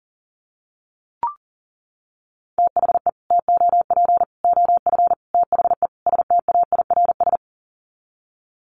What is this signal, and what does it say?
Morse code, heard as a keyed tone, about 40 wpm, 700 Hz. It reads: THE TOP OF THE STAIRS
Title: Sets of 5 Words Encore 40wpm